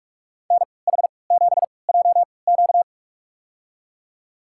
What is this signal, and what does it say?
Morse code, heard as a keyed tone, about 47 wpm, spaced as for 26 wpm, 700 Hz. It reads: NH7JQ
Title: USA Call Signs - 5 Characters - 3x Character Spacing - 45wpm